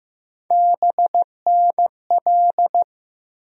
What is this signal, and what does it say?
Morse code, heard as a keyed tone, about 15 wpm, 700 Hz. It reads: BNL
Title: USA Call Sign Postfixes - 3 Letters 15wpm